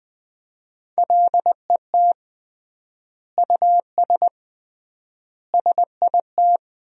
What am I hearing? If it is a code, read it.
Morse code, heard as a keyed tone, about 21 wpm, 700 Hz. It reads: LET US SIT
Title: 3x Word-Spacing - Sentences from Top 500 Words - 20wpm